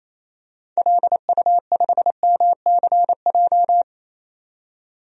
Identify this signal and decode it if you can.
Morse code, heard as a keyed tone, about 28 wpm, 700 Hz. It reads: LU5MCJ
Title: DXCC-100 Call Signs - 6 Characters - 28wpm